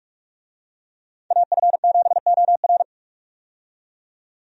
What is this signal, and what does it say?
Morse code, heard as a keyed tone, about 47 wpm, 700 Hz. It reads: AF7OR
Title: USA Call Signs - 5 Characters - 45wpm